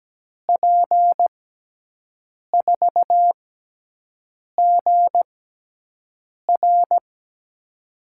Morse code, heard as a keyed tone, about 17 wpm, spaced as for 6 wpm, 700 Hz. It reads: P4GR
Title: USA Call Signs - 5 Characters - 6x Character Spacing - 17wpm